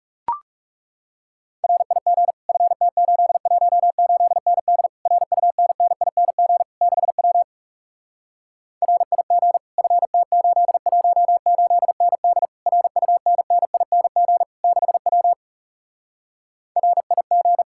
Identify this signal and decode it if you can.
Morse code, heard as a keyed tone, about 40 wpm, 700 Hz. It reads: RIG FT818ND RUNNING 6W RIG FT818ND RUNNING 6W RIG